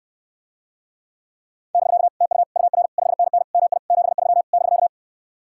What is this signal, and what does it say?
Morse code, heard as a keyed tone, about 67 wpm, 700 Hz. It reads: <BK> TU UR 5NN MI 73 <BK>